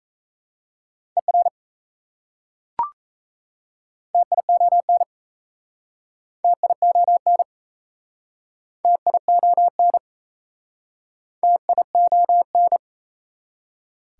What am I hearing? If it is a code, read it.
Morse code, heard as a keyed tone, about 37 wpm, 690 Hz. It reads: ER TION TION TION TION